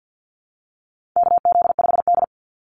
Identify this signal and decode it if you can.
Morse code, heard as a keyed tone, about 50 wpm, 700 Hz. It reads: KZ5D
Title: Call Signs Repeated 2x - 50wpm